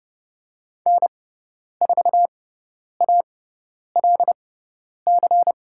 Morse code, heard as a keyed tone, about 30 wpm, spaced as for 10 wpm, 700 Hz. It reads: N4ALC
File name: Lesson 17g - Letter u - Call Signs 30wpm